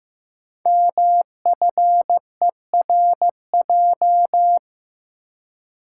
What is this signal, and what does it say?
Morse code, heard as a keyed tone, about 15 wpm, 700 Hz. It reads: MFERJ
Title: Code Groups - 5 Letters - 15wpm